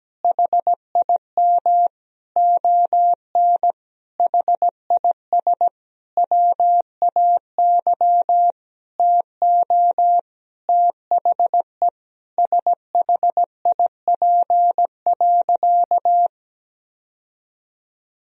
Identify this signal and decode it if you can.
Morse code, heard as a keyed tone, about 17 wpm, 700 Hz. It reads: HIM ON HIS WAY TO THE SHIP.
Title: Sentences from Top 1000 Words - Review Part 2 - 17wpm